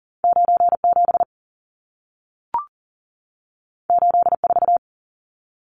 Morse code, heard as a keyed tone, about 40 wpm, 700 Hz. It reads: 97 84
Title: Numbers - 2-Digit 40wpm